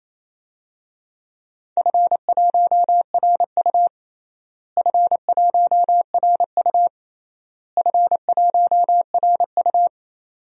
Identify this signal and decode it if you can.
Morse code, heard as a keyed tone, about 28 wpm, 700 Hz. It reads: F1RU F1RU F1RU
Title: DXCC-100 Call Signs - 4 Characters - Repeated 3x - 28wpm